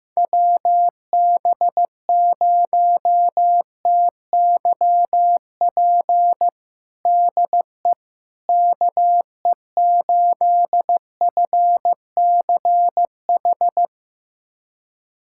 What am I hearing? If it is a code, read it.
Morse code, heard as a keyed tone, about 15 wpm, 700 Hz. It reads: WB0TYP DE KE8FCH
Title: USA Callsign DE Callsign - 6 Characters - 15wpm